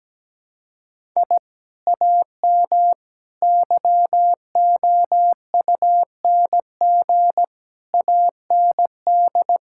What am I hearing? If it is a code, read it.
Morse code, heard as a keyed tone, about 17 wpm, 700 Hz. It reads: I AM YOUNG AND
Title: Sentences from Top 1000 Words - Review Part 5 - 17wpm